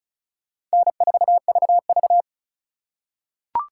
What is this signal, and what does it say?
Morse code, heard as a keyed tone, about 34 wpm, 700 Hz. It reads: N4VV
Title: USA Call Signs - 2x Character Spacing - 35wpm